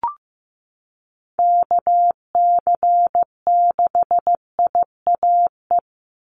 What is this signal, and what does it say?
Morse code, heard as a keyed tone, about 15 wpm, 700 Hz. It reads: KC6IAE
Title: USA Call Signs 15wpm